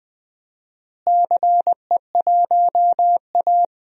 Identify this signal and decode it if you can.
Morse code, heard as a keyed tone, about 20 wpm, 700 Hz. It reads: CE1A
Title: DXCC-100 Call Signs - 4 Characters - Repeated 2x - 20wpm